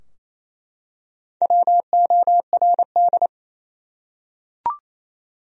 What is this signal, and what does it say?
Morse code, heard as a keyed tone, about 28 wpm, 700 Hz. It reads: WORD
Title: Top 100 Words Repeated 28wpm